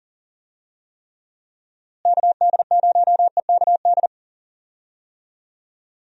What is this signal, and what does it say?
Morse code, heard as a keyed tone, about 40 wpm, 700 Hz. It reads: KD0EKD
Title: USA Call Signs - 6 Characters - Repeated 3x - 40wpm